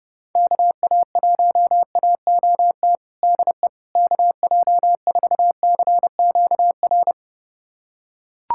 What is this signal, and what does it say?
Morse code, heard as a keyed tone, about 30 wpm, 700 Hz. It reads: KA1AOT DE KJ4CQR
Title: USA Callsign DE Callsign 30wpm